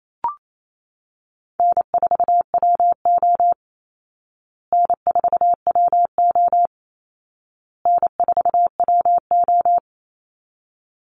Morse code, heard as a keyed tone, about 28 wpm, 700 Hz. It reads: N4WO N4WO N4WO
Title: Call Signs Repeated 2x - 28wpm